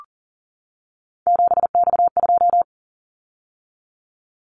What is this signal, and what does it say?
Morse code, heard as a keyed tone, about 38 wpm, 700 Hz. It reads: 7X2
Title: Callsign Prefixes for Top 100 DXCC - 40wpm